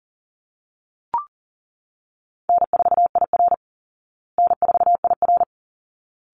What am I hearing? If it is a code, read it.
Morse code, heard as a keyed tone, about 40 wpm, 700 Hz. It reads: N4IR N4IR
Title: Easy Call Signs Repeated 2x - 40wpm